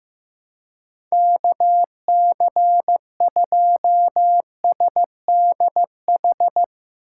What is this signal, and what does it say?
Morse code, heard as a keyed tone, about 15 wpm, 700 Hz. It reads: KC2SDH
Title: USA Call Signs - 2x Character Spacing - 15wpm